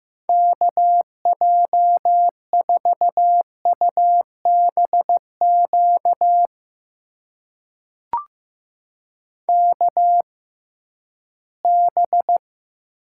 Morse code, heard as a keyed tone, about 15 wpm, spaced as for 11 wpm, 700 Hz. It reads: KJ4UBQ KB
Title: USA Call Signs - 6 Characters - 6x Character Spacing - 15wpm